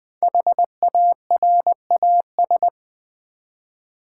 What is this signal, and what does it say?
Morse code, heard as a keyed tone, about 20 wpm, 700 Hz. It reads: HARAS
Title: Build A Word - 5 Letters At A Time - 10-Letter Words - 20wpm